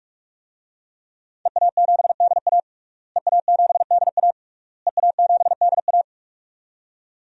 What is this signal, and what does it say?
Morse code, heard as a keyed tone, about 46 wpm, 700 Hz. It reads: EA7DA EA7DA EA7DA
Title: DXCC-100 Call Signs - 5 Characters - Repeated 3x - 45wpm